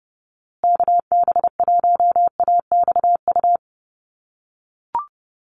Morse code, heard as a keyed tone, about 30 wpm, 700 Hz. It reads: KB1AXU
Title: USA Call Signs Repeated 3x 30wpm